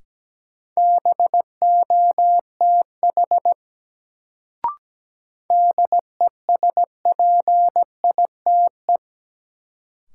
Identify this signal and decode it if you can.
Morse code, heard as a keyed tone, about 17 wpm, 700 Hz. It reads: BOTH DESPITE
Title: Sets of 4 Words - New Word Drill - 17wpm